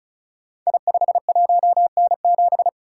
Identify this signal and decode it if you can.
Morse code, heard as a keyed tone, about 33 wpm, 700 Hz. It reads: I51N7
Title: Code Groups - 5 Letter:Numbers - 35wpm